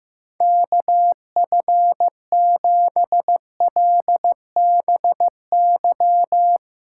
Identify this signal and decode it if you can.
Morse code, heard as a keyed tone, about 15 wpm, 700 Hz. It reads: KF7LBY
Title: USA Call Signs - 6 Characters - 8x Character Spacing - 15wpm